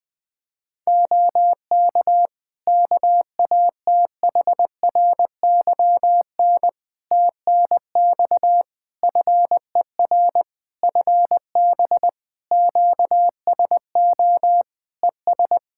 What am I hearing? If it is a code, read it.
Morse code, heard as a keyed tone, about 21 wpm, 700 Hz. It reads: OK KATHRYN TNX FER FB QSO ES
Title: K1USN SST QSO Practice 20wpm